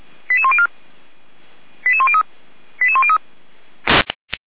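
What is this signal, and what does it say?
Five-tone selective call (ZVEI), heard as quick runs of five tones, tones of about 70 ms. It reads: 88104 88103 88103
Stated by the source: ZVEI five-tone call received by SDR, Feuerwehr Orlamünde.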